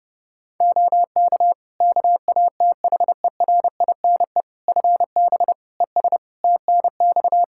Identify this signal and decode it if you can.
Morse code, heard as a keyed tone, about 29 wpm, 700 Hz. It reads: OK KATHERINE FB ES TNX
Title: K1USN SST QSO Practice 30wpm